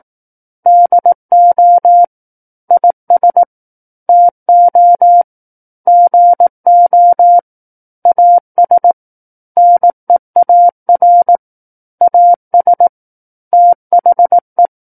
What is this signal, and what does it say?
Morse code, heard as a keyed tone, about 18 wpm, 700 Hz. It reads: DO IS TO GO AS NEAR AS THE
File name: 18 WPM